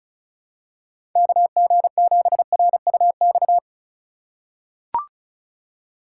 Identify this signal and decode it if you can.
Morse code, heard as a keyed tone, about 34 wpm, 700 Hz. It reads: KG7RUX
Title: USA Call Signs - 2x Character Spacing - 35wpm